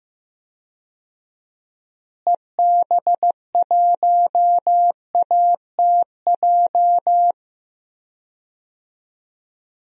Morse code, heard as a keyed tone, about 15 wpm, 700 Hz. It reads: EB1ATJ
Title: DXCC-100 Call Signs - 6 Characters - Repeated 2x - 15wpm